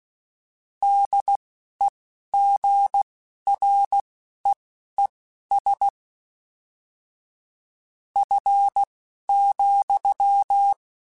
Morse code, heard as a keyed tone, about 16 wpm, spaced as for 12 wpm, 800 Hz. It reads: DEGREES F,
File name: Weather Sets / Weather Sets-mp3s / Weather 5 1 @10